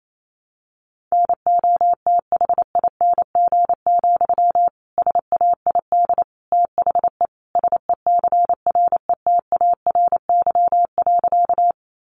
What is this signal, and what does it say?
Morse code, heard as a keyed tone, about 28 wpm, 700 Hz. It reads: NOTHING, SAID THE SECRETARY.
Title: Sentences from Top 1300 Words 28wpm